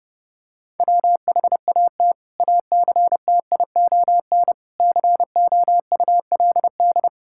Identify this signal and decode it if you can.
Morse code, heard as a keyed tone, about 30 wpm, 700 Hz. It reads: WHAT ACTION COULD
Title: Sentences from Top 1000 Words - Review Part 6 - 30wpm